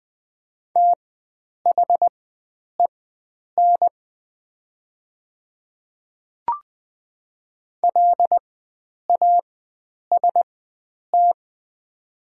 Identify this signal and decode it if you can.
Morse code, heard as a keyed tone, about 20 wpm, spaced as for 9 wpm, 700 Hz. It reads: THEN LAST
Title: 4x Character Spacing - Top 300 Words - 20wpm